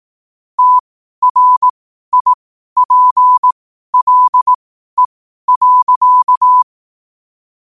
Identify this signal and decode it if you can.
Morse code, heard as a keyed tone, about 18 wpm, spaced as for 12 wpm, 1000 Hz. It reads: TRIPLE.
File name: qotdfiles / 12wpm